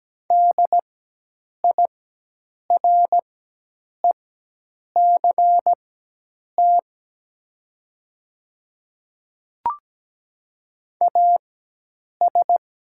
Morse code, heard as a keyed tone, about 17 wpm, spaced as for 8 wpm, 700 Hz. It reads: DIRECT AS
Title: 4x Character Spacing - Top 500 Words - 17wpm